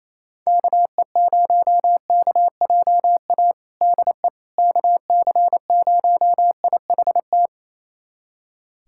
Morse code, heard as a keyed tone, about 28 wpm, 700 Hz. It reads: KE0KJA DE KC0IHT
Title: USA Callsign DE Callsign - 6 Characters - 28wpm